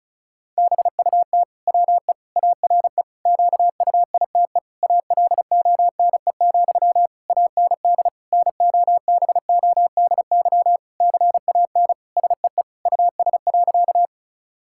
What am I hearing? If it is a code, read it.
Morse code, heard as a keyed tone, about 34 wpm, 700 Hz. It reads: BUT WE ARE QUITE ALONE, AND NOBODY CAN SEE US.
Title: Sentences from Top 1600 Words 35wpm